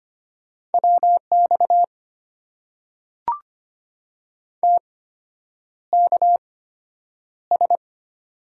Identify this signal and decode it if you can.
Morse code, heard as a keyed tone, about 25 wpm, spaced as for 9 wpm, 700 Hz. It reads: WX TKS
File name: Top 90 QSO Elements - 8x Character Spacing - 25wpm